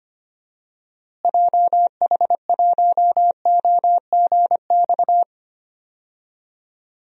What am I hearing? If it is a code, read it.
Morse code, heard as a keyed tone, about 25 wpm, 700 Hz. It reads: JH1OGX
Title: DXCC-100 Call Signs - 6 Characters - Repeated 3x - 25wpm